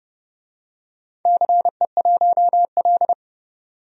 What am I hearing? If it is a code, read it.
Morse code, heard as a keyed tone, about 30 wpm, 700 Hz. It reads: CE1L
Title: DXCC-100 Call Signs - 4 Characters - 30wpm